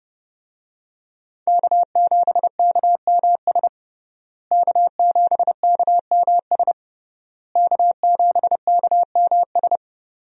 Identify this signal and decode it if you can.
Morse code, heard as a keyed tone, about 30 wpm, 700 Hz. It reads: K7KMS K7KMS K7KMS
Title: USA Call Signs - 5 Characters - Repeated 3x - 30wpm